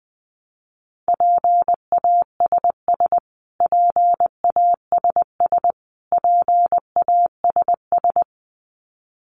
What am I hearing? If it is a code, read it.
Morse code, heard as a keyed tone, about 20 wpm, 700 Hz. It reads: PASS PASS PASS
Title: Top 500 Words Repeated 20wpm